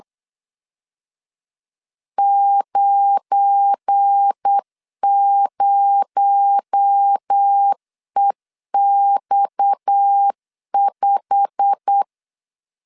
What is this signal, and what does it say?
Morse code, heard as a keyed tone, about 8 wpm, 800 Hz. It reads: 90EX5
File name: LEZIONI / 9 2867 8wpm 3440